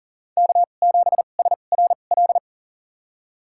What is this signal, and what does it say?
Morse code, heard as a keyed tone, about 40 wpm, spaced as for 29 wpm, 700 Hz. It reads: K7SRL
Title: USA Call Signs - 5 Characters - 2x Character Spacing - 40wpm